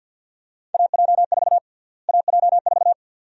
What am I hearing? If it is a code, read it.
Morse code, heard as a keyed tone, about 51 wpm, 700 Hz. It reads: AJ4 AJ4
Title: USA Call Sign Prefixes - Repeated 2x - 50wpm